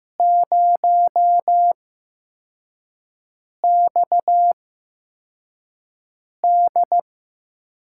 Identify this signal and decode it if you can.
Morse code, heard as a keyed tone, about 15 wpm, spaced as for 4 wpm, 700 Hz. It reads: 0XD
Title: USA Call Signs - 5 Characters - 8x Character Spacing - 15wpm